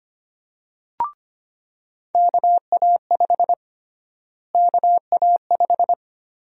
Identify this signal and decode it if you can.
Morse code, heard as a keyed tone, about 25 wpm, 700 Hz. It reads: KA5 KA5
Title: USA Call Sign Prefixes - Repeated 2x - 25wpm